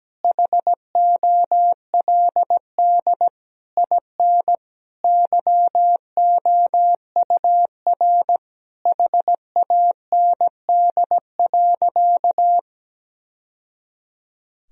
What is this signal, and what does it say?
Morse code, heard as a keyed tone, about 17 wpm, 700 Hz. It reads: HOLD IN YOUR HAND.
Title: Sentences from Top 1000 Words - Review Part 2 - 17wpm